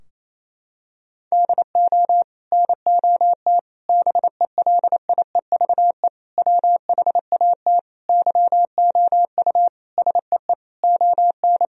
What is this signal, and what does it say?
Morse code, heard as a keyed tone, about 28 wpm, 700 Hz. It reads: DO NOT BELIEVE WHAT YOU SEE ON
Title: Sentences from Top 1000 Words - Review Part 4 - 28wpm